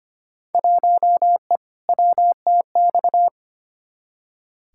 Morse code, heard as a keyed tone, about 25 wpm, 700 Hz. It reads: 1E WTX
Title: Field Day Exchanges - 25wpm